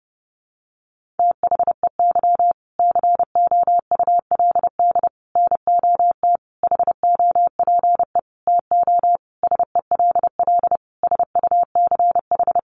Morse code, heard as a keyed tone, about 30 wpm, 700 Hz. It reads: THEY COULD NOT HOPE TO SELL SUCH
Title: Sentences from Top 1100 Words 30wpm